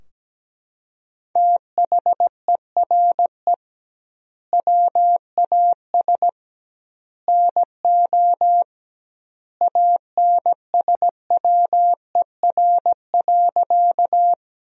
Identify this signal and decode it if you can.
Morse code, heard as a keyed tone, about 17 wpm, 700 Hz. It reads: THERE WAS NO ANSWER.
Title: 2x Word-Spacing - Sentences from Top 300 Words - 17wpm